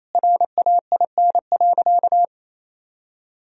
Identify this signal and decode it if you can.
Morse code, heard as a keyed tone, about 28 wpm, 700 Hz. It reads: RAIN.
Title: Sentences from Top 1000 Words - Review Part 2 - 28wpm